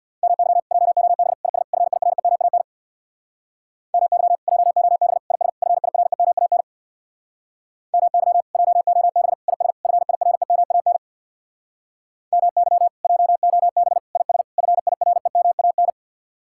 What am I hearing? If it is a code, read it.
Morse code, heard as a keyed tone, about 66 wpm, 700 Hz. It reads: MY JOB IS FIREMAN MY JOB IS FIREMAN MY JOB IS FIREMAN MY JOB IS FIREMAN